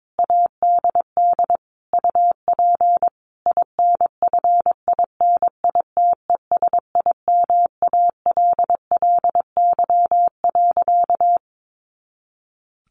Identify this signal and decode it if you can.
Morse code, heard as a keyed tone, about 22 wpm, 700 Hz. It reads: ADD UP INFINITESIMALLY.
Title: A Sound Of Thunder 22wpm